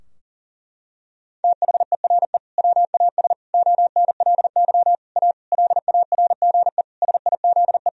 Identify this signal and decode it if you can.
Morse code, heard as a keyed tone, about 40 wpm, 700 Hz. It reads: THERE WAS ONLY A LARGE SIZE
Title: Sentences from Top 1000 Words - Review Part 2 - 40wpm